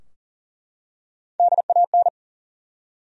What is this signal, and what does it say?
Morse code, heard as a keyed tone, about 40 wpm, 700 Hz. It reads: DAN
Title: USA Common Names - Max 4 Letters - 40wpm